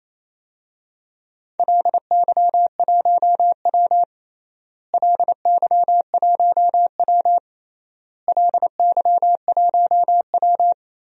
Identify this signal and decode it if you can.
Morse code, heard as a keyed tone, about 28 wpm, 700 Hz. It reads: LY1W LY1W LY1W
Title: DXCC-100 Call Signs - 4 Characters - Repeated 3x - 28wpm